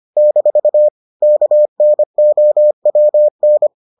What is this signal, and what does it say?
Morse code, heard as a keyed tone, about 25 wpm, 600 Hz. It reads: - KNOWN